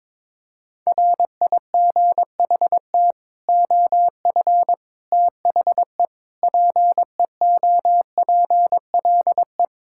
Morse code, heard as a keyed tone, about 22 wpm, 700 Hz. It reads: RIGHT OF THE PEOPLE
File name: practice / 22wpm / Sets of 8 Words - 22wpm